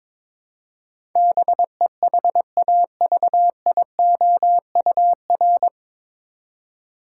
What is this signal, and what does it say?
Morse code, heard as a keyed tone, about 22 wpm, 700 Hz. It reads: BEHAVIOUR
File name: 9 Letter Words - First 3 Letters Given - 22wpm